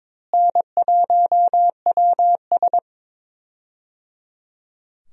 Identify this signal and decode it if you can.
Morse code, heard as a keyed tone, about 22 wpm, 700 Hz. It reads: N1WS
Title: USA Call Signs - 4 Characters - Repeated 3x - 22wpm